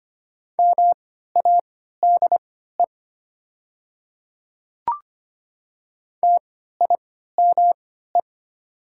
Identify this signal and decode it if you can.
Morse code, heard as a keyed tone, about 25 wpm, spaced as for 14 wpm, 700 Hz. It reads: MADE TIME
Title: 3x Character Spacing - Top 200 Words - 25wpm